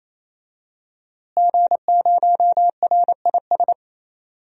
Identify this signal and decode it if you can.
Morse code, heard as a keyed tone, about 28 wpm, 700 Hz. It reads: G0RIS
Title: DXCC-100 Call Signs - 5 Characters - Repeated 3x - 28wpm